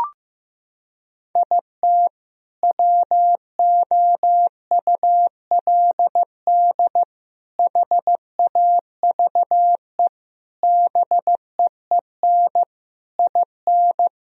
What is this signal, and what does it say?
Morse code, heard as a keyed tone, about 15 wpm, 700 Hz. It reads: IT WOULD HAVE BEEN IN
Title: Sentences from Top 1000 Words - Review Part 1 - 15wpm